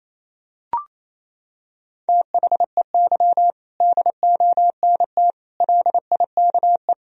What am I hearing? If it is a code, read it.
Morse code, heard as a keyed tone, about 28 wpm, 700 Hz. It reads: THEY DONT LIKE
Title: Sentences from Top 1000 Words - Review Part 2 - 28wpm